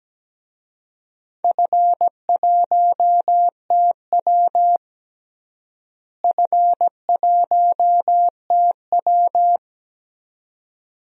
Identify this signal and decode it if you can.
Morse code, heard as a keyed tone, about 17 wpm, 700 Hz. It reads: F1TW F1TW
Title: DXCC-100 Call Signs - 4 Characters - Repeated 2x - 17wpm